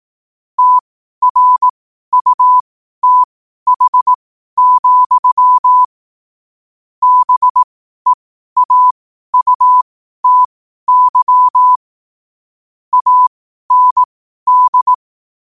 Morse code, heard as a keyed tone, about 18 wpm, spaced as for 12 wpm, 1000 Hz. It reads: TRUTH, BEAUTY AND
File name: qotdfiles / 12wpm